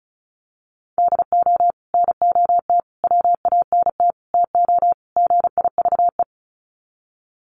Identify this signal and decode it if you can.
Morse code, heard as a keyed tone, about 34 wpm, 700 Hz. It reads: DO NOT WANT TO GIVE
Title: Sets of 5 Words Encore 35wpm